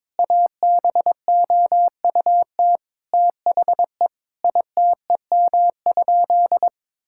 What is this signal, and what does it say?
Morse code, heard as a keyed tone, about 22 wpm, 700 Hz. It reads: ABOUT THE ITEM?
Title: Sentences from Top 1000 Words - Review Part 6 - 22wpm